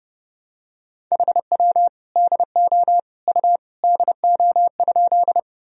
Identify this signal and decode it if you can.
Morse code, heard as a keyed tone, about 30 wpm, 700 Hz. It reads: HW DO U DO?
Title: Sentences from Top 100 Words and Abbreviations 30wpm